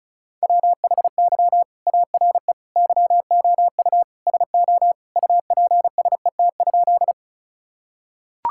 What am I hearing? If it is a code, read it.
Morse code, heard as a keyed tone, about 36 wpm, 700 Hz. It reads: WHY ARE YOU SO UPSET?
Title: Sentences from Top 1000 Words - Review Part 5 - 35wpm